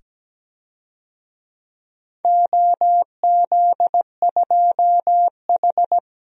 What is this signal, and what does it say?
Morse code, heard as a keyed tone, about 17 wpm, 700 Hz. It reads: OZ2H
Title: DXCC-100 Call Signs - 4 Characters - 17wpm